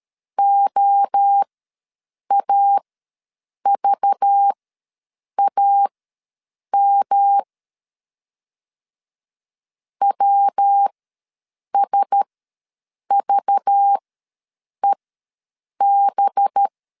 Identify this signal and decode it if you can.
Morse code, heard as a keyed tone, about 13 wpm, spaced as for 7 wpm, 800 Hz. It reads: OAVAM WSVEB
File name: LEZIONI / 5 LBVC / 5 LBV ws 12wpm 3015